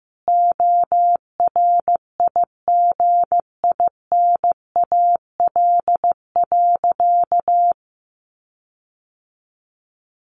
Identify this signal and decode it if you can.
Morse code, heard as a keyed tone, about 15 wpm, 700 Hz. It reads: ORIGINAL.
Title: Sentences from Top 1400 Words 15wpm